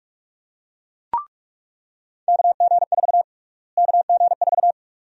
Morse code, heard as a keyed tone, about 44 wpm, 700 Hz. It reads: KG4 KG4
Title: USA Call Sign Prefixes - Repeated 2x - 45wpm